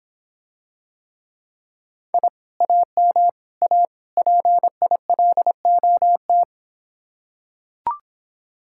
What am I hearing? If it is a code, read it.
Morse code, heard as a keyed tone, about 26 wpm, 700 Hz. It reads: I AM A PILOT